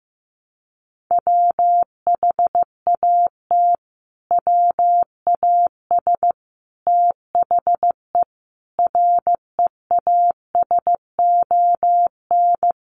Morse code, heard as a keyed tone, about 15 wpm, 700 Hz. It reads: WHAT WAS THE REASON
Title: Sentences from Top 700 Words 15wpm